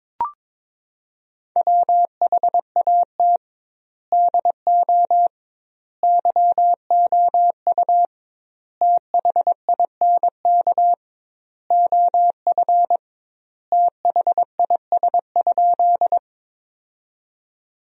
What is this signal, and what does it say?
Morse code, heard as a keyed tone, about 22 wpm, 700 Hz. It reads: WHAT DO YOU THINK OF THIS?
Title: 2x Word-Spacing - Sentences from Top 200 Words - 22wpm